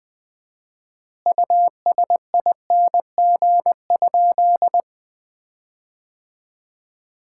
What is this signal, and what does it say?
Morse code, heard as a keyed tone, about 20 wpm, 700 Hz. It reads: USING?
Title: Common QSO Phrases - 4x Word-Spacing - 20wpm